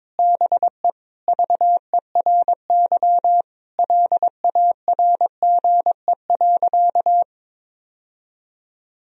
Morse code, heard as a keyed tone, about 22 wpm, 700 Hz. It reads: BE VERY LARGE.
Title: Sentences from Top 1000 Words - Review Part 1 - 22wpm